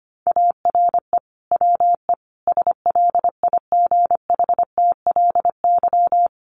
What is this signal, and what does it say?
Morse code, heard as a keyed tone, about 25 wpm, 700 Hz. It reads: ARE WE SLIGHTLY
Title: Sentences from Top 1300 Words 25wpm